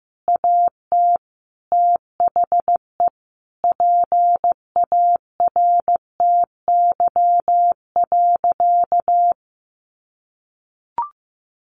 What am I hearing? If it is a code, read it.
Morse code, heard as a keyed tone, about 15 wpm, 700 Hz. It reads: AT THE PARTY.
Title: Sentences from Top 1400 Words 15wpm